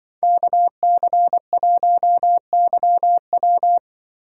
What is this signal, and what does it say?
Morse code, heard as a keyed tone, about 25 wpm, 700 Hz. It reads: KC1YW
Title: Speed Racing - USA Call Signs 20wpm